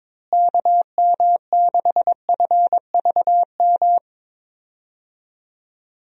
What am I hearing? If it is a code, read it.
Morse code, heard as a keyed tone, about 22 wpm, 700 Hz. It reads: KM6FVM
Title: USA Call Signs - 6 Characters - Repeated 2x - 22wpm